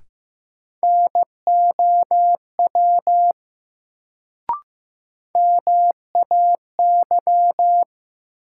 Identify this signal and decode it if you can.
Morse code, heard as a keyed tone, about 15 wpm, 700 Hz. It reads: NOW MAY
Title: Sentences from Top 100 Words - New Word Drill - 15wpm